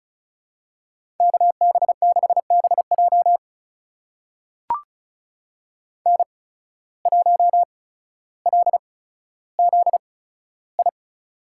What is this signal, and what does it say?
Morse code, heard as a keyed tone, about 34 wpm, spaced as for 15 wpm, 700 Hz. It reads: KB6BJ N1LZI
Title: USA Call Signs - 5 Characters - 8x Character Spacing - 35wpm